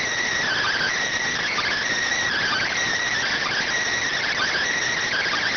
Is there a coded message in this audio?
Yes: Emergency Alert System (SAME) tones from 0.4 to 4.6 s, heard as harsh buzzing bursts.